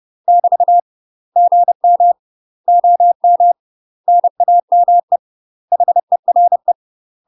Morse code, heard as a keyed tone, about 30 wpm, 700 Hz. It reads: <BT> GM OM NAME HERE
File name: UK MASTER QSOs TRAINING / 30WPM / UK MASTER QSOs 0022